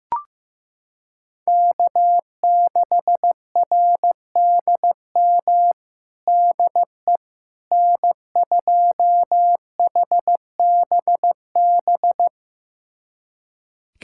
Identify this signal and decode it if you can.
Morse code, heard as a keyed tone, about 15 wpm, 700 Hz. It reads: K6RDM DE N2HBB
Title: USA Callsign DE Callsign - 5 Characters - 15wpm